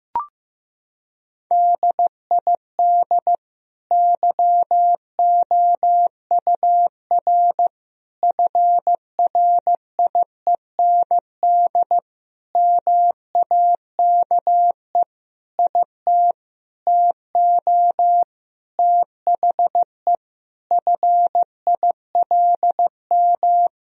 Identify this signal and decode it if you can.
Morse code, heard as a keyed tone, about 15 wpm, 700 Hz. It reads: DID YOUR FRIEND MAKE IT TO THE FILM
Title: Sentences from Top 1000 Words - Review Part 3 - 15wpm